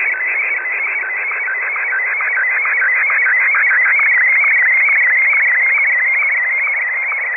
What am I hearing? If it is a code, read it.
SSTV (slow-scan television), Martin 1.
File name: luxorion / Radio / mode-sstv-martin1-9h